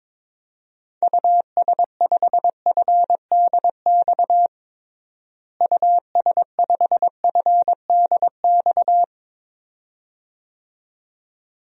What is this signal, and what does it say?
Morse code, heard as a keyed tone, about 22 wpm, 700 Hz. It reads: US5FDX US5FDX
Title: DXCC-100 Call Signs - 6 Characters - Repeated 2x - 22wpm